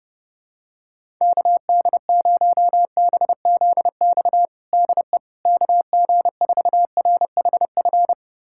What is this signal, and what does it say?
Morse code, heard as a keyed tone, about 31 wpm, 700 Hz. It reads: KD0BZX DE KG4RHF
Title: USA Callsign DE Callsign - 6 Characters - 30wpm